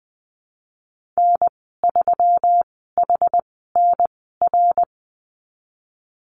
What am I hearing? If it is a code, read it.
Morse code, heard as a keyed tone, about 20 wpm, spaced as for 15 wpm, 700 Hz. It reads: N3HNR
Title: 173 Lesson 28g - Number 9 - Call Signs 20wpm